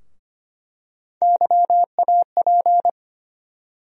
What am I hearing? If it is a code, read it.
Morse code, heard as a keyed tone, about 25 wpm, 700 Hz. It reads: YAP
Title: USA Call Sign Postfixes - 3 Letters 25wpm